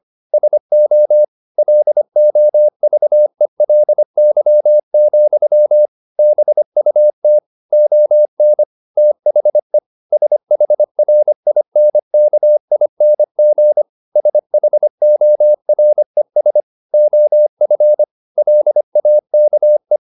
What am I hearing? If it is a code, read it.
Morse code, heard as a keyed tone, about 25 wpm, 600 Hz. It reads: SO LOVELY, BUT ON THE SHRINKING SHORES OF LAKE